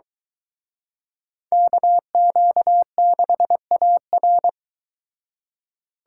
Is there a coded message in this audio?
Yes: Morse code from 1.5 to 4.5 s, heard as a keyed tone, about 23 wpm, 700 Hz. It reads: KQ6AR